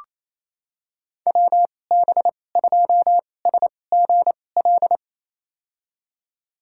Morse code, heard as a keyed tone, about 28 wpm, spaced as for 20 wpm, 700 Hz. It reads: WB2SGL
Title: USA Call Signs - 6 Characters - 2x Character Spacing - 28wpm